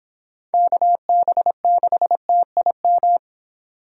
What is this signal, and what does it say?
Morse code, heard as a keyed tone, about 25 wpm, 700 Hz. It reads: KB6TIM